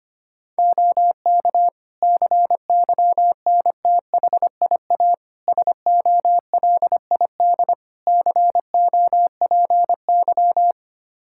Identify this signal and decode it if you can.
Morse code, heard as a keyed tone, about 25 wpm, 700 Hz. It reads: OK CYNTHIA SOLID COPY
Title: Practice QSO - Exchange 3 - More Info - 25wpm